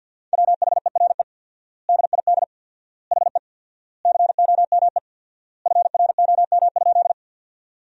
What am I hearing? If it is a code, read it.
Morse code, heard as a keyed tone, about 51 wpm, 700 Hz. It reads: WHERE DID HE COME FROM?
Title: 4x Word-Spacing - Sentences from Top 200 Words - 50wpm